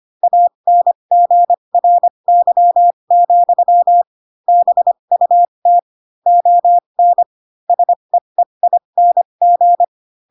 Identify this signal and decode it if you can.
Morse code, heard as a keyed tone, about 25 wpm, spaced as for 22 wpm, 700 Hz. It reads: ANGRY, BUT ON SEEING
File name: Pinocchio / Pinocchio ch7 18 25 ik1hge